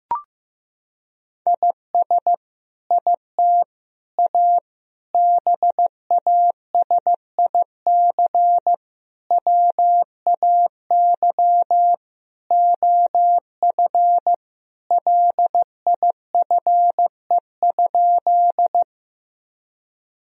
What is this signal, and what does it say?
Morse code, heard as a keyed tone, about 16 wpm, 700 Hz. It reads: IS IT A BASIC WAY OF LIFE?
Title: Sentences from Top 1000 Words - Review Part 3 - 15wpm